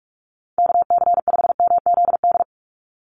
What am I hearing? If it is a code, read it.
Morse code, heard as a keyed tone, about 44 wpm, 700 Hz. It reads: KC5MZD
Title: USA Call Signs 45wpm